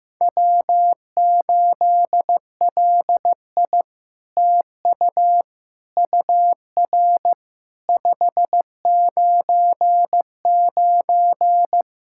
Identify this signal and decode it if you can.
Morse code, heard as a keyed tone, about 15 wpm, 700 Hz. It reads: W8LI TU UR 599
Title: SOTA QSO Practice 15wpm